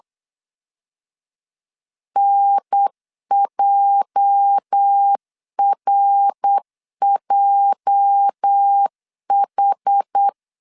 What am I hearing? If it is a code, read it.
Morse code, heard as a keyed tone, about 8 wpm, 800 Hz. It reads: NJRJH
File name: LEZIONI / 8 YJ45 / 8 YJ 8wpm 3313